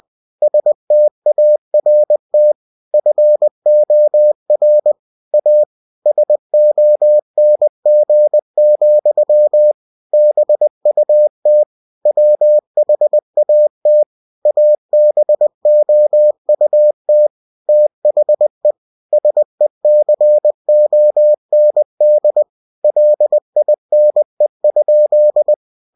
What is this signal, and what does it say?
Morse code, heard as a keyed tone, about 20 wpm, 600 Hz. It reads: START FOR A SONG, BUT WHAT ABOUT THE SECOND LINE?